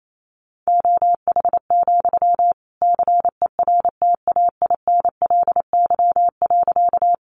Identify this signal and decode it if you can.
Morse code, heard as a keyed tone, about 28 wpm, 700 Hz. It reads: OH, CERTAINLY.
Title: Sentences from Top 900 Words 28wpm